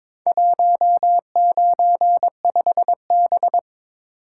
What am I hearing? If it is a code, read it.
Morse code, heard as a keyed tone, about 22 wpm, 700 Hz. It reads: 195B
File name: practice / 22wpm / Four Letter-Number Combinations - 22wpm